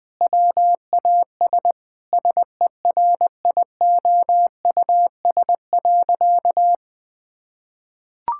Morse code, heard as a keyed tone, about 20 wpm, 700 Hz. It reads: WAS SERIOUS.
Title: Sentences from Top 1600 Words 20wpm